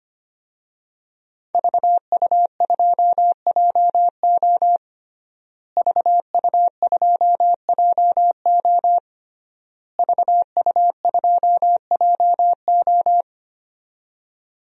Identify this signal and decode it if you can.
Morse code, heard as a keyed tone, about 25 wpm, 700 Hz. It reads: VU2JO VU2JO VU2JO
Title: DXCC-100 Call Signs - 5 Characters - Repeated 3x - 25wpm